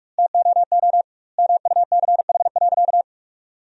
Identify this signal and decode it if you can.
Morse code, heard as a keyed tone, about 41 wpm, 700 Hz. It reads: TOO MUCH.
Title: 2x Word-Spacing - Sentences from Top 200 Words - 45wpm